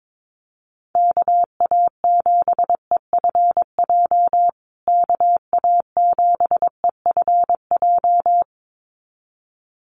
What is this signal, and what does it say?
Morse code, heard as a keyed tone, about 22 wpm, 700 Hz. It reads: KA7EFJ KA7EFJ
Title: USA Call Signs Repeated 2x 22wpm